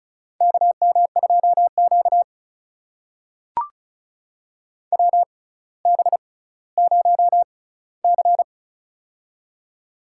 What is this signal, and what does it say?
Morse code, heard as a keyed tone, about 36 wpm, spaced as for 18 wpm, 700 Hz. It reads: KM2Q WB0C
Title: USA Call Signs - 4 Characters - 6x Character Spacing - 35wpm